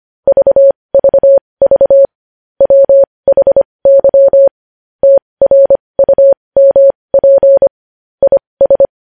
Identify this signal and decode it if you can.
Morse code, heard as a keyed tone, about 25 wpm, spaced as for 20 wpm, 550 Hz. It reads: VVV WHY TRUMP IS
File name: podcast / News headlines-1x-25x20